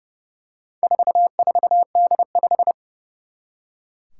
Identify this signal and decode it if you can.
Morse code, heard as a keyed tone, about 28 wpm, 700 Hz. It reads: 44D5
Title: Four Challenging Letter-Number Combinations 30wpm